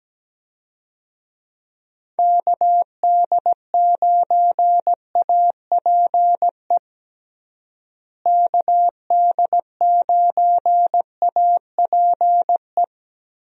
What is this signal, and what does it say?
Morse code, heard as a keyed tone, about 17 wpm, 700 Hz. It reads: KD9APE KD9APE
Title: USA Call Signs - 6 Characters - Repeated 2x - 17wpm